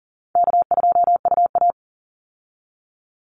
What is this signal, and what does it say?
Morse code, heard as a keyed tone, about 40 wpm, 700 Hz. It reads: K2UA
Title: Call Signs Repeated 3x - 40wpm